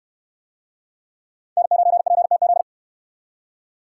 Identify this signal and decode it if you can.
Morse code, heard as a keyed tone, about 64 wpm, 700 Hz. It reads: N9PTZ